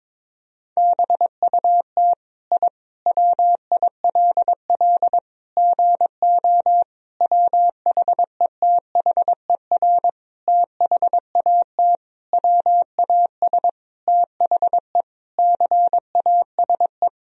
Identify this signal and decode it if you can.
Morse code, heard as a keyed tone, about 21 wpm, 700 Hz. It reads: BUT I WILL GO WHETHER THAT WAS THE CASE
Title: Sentences from Top 1000 Words - Review Part 3 - 22wpm